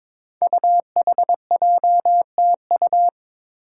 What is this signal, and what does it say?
Morse code, heard as a keyed tone, about 23 wpm, 700 Hz. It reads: UHJTU
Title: Five Challenging Letter-Number Combinations 22wpm